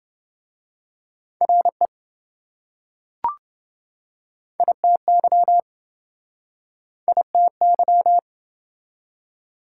Morse code, heard as a keyed tone, about 25 wpm, 700 Hz. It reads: RE ITY ITY